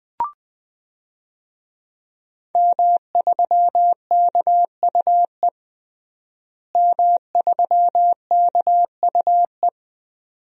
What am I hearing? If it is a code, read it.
Morse code, heard as a keyed tone, about 20 wpm, 700 Hz. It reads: M3KUE M3KUE
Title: DXCC-100 Call Signs - 5 Characters - Repeated 2x - 20wpm